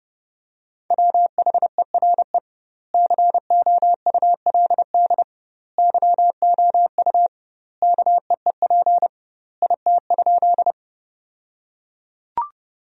Morse code, heard as a keyed tone, about 30 wpm, 700 Hz. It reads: WHERE COULD YOU KEEP IT?
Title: 2x Word-Spacing - Sentences from Top 300 Words - 30wpm